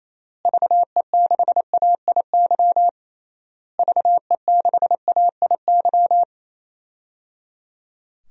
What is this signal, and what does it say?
Morse code, heard as a keyed tone, about 27 wpm, 700 Hz. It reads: VE6AIY VE6AIY
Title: DXCC-100 Call Signs - 6 Characters - Repeated 3x - 28wpm